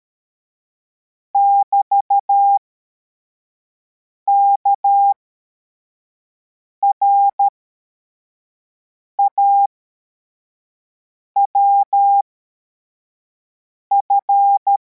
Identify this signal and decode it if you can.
Morse code, heard as a keyed tone, about 13 wpm, spaced as for 4 wpm, 800 Hz. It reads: <BT> KRAWF
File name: formation / cw / Text61